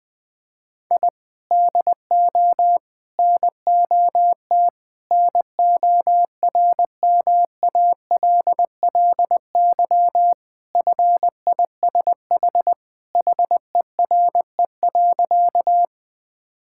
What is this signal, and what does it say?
Morse code, heard as a keyed tone, about 21 wpm, 700 Hz. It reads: I DO NOT NORMALLY FISH HERE.
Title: Sentences from Top 1000 Words - Review Part 6 - 20wpm